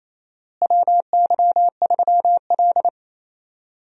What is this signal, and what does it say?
Morse code, heard as a keyed tone, about 28 wpm, 700 Hz. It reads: WY3L
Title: USA Call Signs - 4 Characters - Repeated 3x - 28wpm